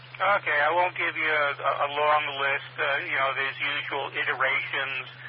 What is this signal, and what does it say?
Emergency Alert System (SAME) tones, heard as harsh buzzing bursts, with speech. It is heard from 0.9 to 4.8 s.